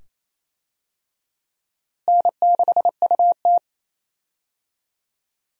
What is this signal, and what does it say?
Morse code, heard as a keyed tone, about 28 wpm, 700 Hz. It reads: N6UT
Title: USA Call Signs - 4 Characters - Repeated 2x - 28wpm